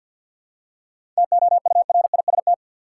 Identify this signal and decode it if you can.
Morse code, heard as a keyed tone, about 50 wpm, 700 Hz. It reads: TOURIST
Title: Seven Letter Words - 50wpm